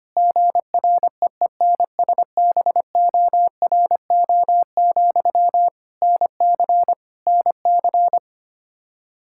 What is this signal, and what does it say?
Morse code, heard as a keyed tone, about 25 wpm, 700 Hz. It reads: GREENSBORO, NC NC
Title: K1USN SST QSO Practice 25wpm